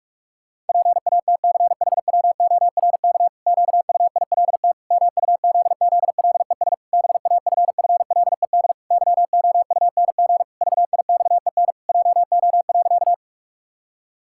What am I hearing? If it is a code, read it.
Morse code, heard as a keyed tone, about 41 wpm, 700 Hz. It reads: PATCHWORK QUILT MUZZLES BAFFLED YOUNG VIXEN JO.
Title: Pangram Challenge v2 - 45wpm